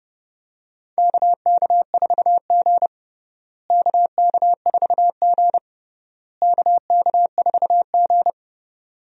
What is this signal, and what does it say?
Morse code, heard as a keyed tone, about 30 wpm, 700 Hz. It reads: KK4G KK4G KK4G
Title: USA Call Signs - 4 Characters - Repeated 3x - 30wpm